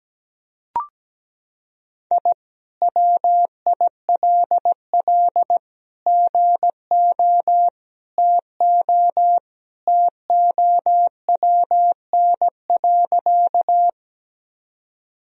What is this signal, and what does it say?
Morse code, heard as a keyed tone, about 17 wpm, 700 Hz. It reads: I WILL GO TO TOWN.
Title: Sentences from Top 1000 Words - Review Part 2 - 17wpm